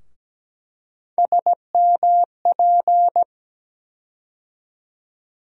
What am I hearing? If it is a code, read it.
Morse code, heard as a keyed tone, about 17 wpm, 700 Hz. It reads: SMP
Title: USA Call Sign Postfixes - 3 Letters 17wpm